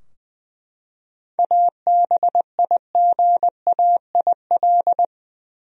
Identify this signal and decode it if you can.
Morse code, heard as a keyed tone, about 20 wpm, 700 Hz. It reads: ABIGAIL
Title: USA Common Names - 20wpm